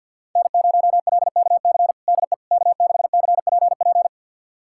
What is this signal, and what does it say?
Morse code, heard as a keyed tone, about 46 wpm, 700 Hz. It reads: N0LKC DE K6CPP
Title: USA Callsign DE Callsign - 5 Characters - 50wpm